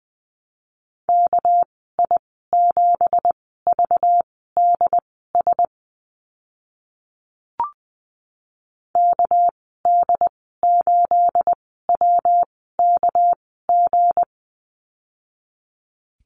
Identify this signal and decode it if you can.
Morse code, heard as a keyed tone, about 20 wpm, spaced as for 14 wpm, 700 Hz. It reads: KI7VDS KD8WKG
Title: 245 Lesson 39g - Letter z - Call Signs 20wpm